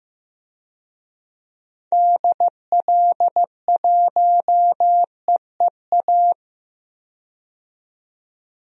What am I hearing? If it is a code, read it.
Morse code, heard as a keyed tone, about 15 wpm, 700 Hz. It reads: DL1EEA
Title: DXCC-100 Call Signs - 6 Characters - 15wpm